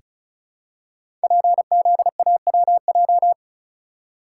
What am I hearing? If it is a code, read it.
Morse code, heard as a keyed tone, about 35 wpm, 700 Hz. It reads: PZAWJ
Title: Code Groups - 5 Letter:Numbers - 35wpm